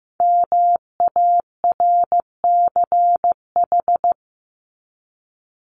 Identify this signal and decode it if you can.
Morse code, heard as a keyed tone, about 15 wpm, 700 Hz. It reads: MARCH
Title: Top 1501-1750 Words 15wpm